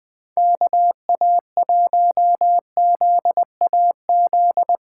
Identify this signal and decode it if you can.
Morse code, heard as a keyed tone, about 20 wpm, 700 Hz. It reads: KA1ZAZ
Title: Speed Racing - USA Call Signs 17wpm